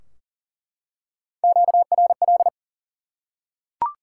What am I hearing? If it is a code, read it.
Morse code, heard as a keyed tone, about 40 wpm, 700 Hz. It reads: QRL
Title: Top 90 QSO Elements 40wpm